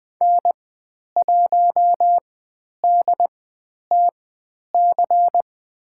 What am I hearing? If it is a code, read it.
Morse code, heard as a keyed tone, about 20 wpm, spaced as for 10 wpm, 700 Hz. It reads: N1DTC
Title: 102 Lesson 17g - Letter u - Call Signs 20wpm